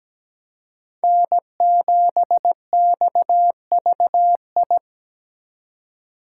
Morse code, heard as a keyed tone, about 20 wpm, 700 Hz. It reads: N7XVI
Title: Speed Racing - USA 1x3 Call Signs 17wpm